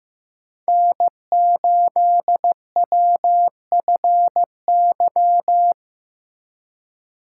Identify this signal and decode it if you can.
Morse code, heard as a keyed tone, about 15 wpm, 700 Hz. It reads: N8WFY
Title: USA Call Signs - 5 Characters - 3x Character Spacing - 15wpm